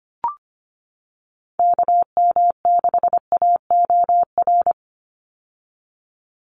Morse code, heard as a keyed tone, about 25 wpm, 700 Hz. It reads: KM6AOR
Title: USA Call Signs 25wpm